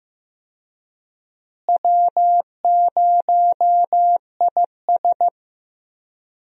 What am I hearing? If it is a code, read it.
Morse code, heard as a keyed tone, about 15 wpm, 700 Hz. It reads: W0IS
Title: USA Call Signs - 4 Characters - Repeated 3x - 15wpm